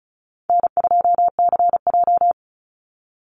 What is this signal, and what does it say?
Morse code, heard as a keyed tone, about 34 wpm, 700 Hz. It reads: N2CJ
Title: Easy Call Signs Repeated 3x - 35wpm